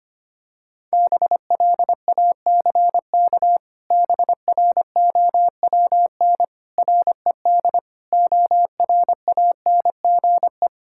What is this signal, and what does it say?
Morse code, heard as a keyed tone, about 25 wpm, 700 Hz. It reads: BLACK BROWN RED ORANGE
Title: Pangram Challenge v2 - 25wpm